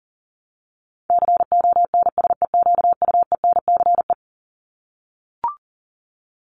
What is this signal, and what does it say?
Morse code, heard as a keyed tone, about 40 wpm, 700 Hz. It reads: CONSEQUENCE
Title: LY Suffix - Top 100 Words with Base Words 40wpm